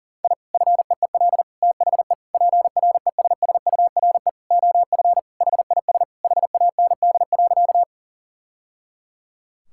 Morse code, heard as a keyed tone, about 40 wpm, 700 Hz. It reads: I FEEL THE PRESSURE OF HIS HAND.
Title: Sentences from Top 1000 Words - Review Part 6 - 40wpm